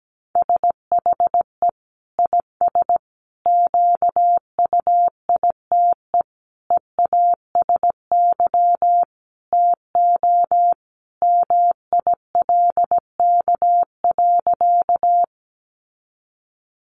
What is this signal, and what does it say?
Morse code, heard as a keyed tone, about 17 wpm, 700 Hz. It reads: SHE IS QUITE EASY TO MILK.
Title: Sentences from Top 1500 Words 17wpm